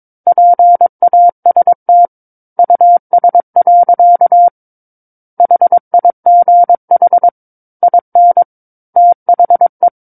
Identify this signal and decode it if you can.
Morse code, heard as a keyed tone, about 22 wpm, 700 Hz. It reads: PAST US. HIGH IN THE
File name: Book One / 13-Chapter 13 - 22 WPM